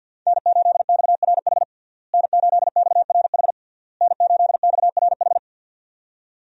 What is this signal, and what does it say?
Morse code, heard as a keyed tone, about 52 wpm, 700 Hz. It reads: N8XRH N8XRH N8XRH
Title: USA Call Signs - 5 Characters - Repeated 3x - 50wpm